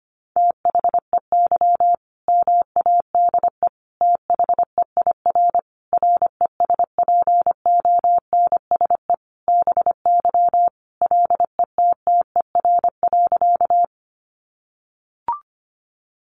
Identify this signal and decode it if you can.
Morse code, heard as a keyed tone, about 25 wpm, 700 Hz. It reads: THEY MADE THEIR RESPONSE BY LETTER.
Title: Sentences from Top 1400 Words 25wpm